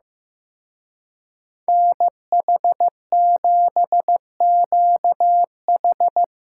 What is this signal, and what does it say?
Morse code, heard as a keyed tone, about 17 wpm, 700 Hz. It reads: NH7QH
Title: Speed Racing - USA 2x2 Call Signs 15wpm